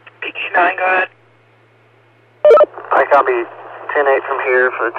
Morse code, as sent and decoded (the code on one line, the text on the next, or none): none